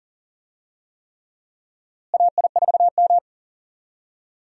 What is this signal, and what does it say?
Morse code, heard as a keyed tone, about 39 wpm, 700 Hz. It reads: AI4M
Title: USA Call Signs - 4 Characters - Repeated 3x - 40wpm